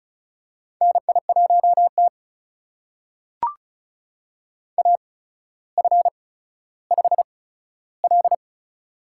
Morse code, heard as a keyed tone, about 36 wpm, spaced as for 15 wpm, 700 Hz. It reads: NI1T AF5L
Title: USA Call Signs - 4 Characters - 8x Character Spacing - 35wpm